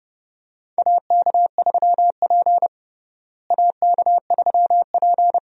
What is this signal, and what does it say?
Morse code, heard as a keyed tone, about 32 wpm, 700 Hz. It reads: AK3P AK3P
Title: USA Call Signs - 4 Characters - Repeated 2x - 30wpm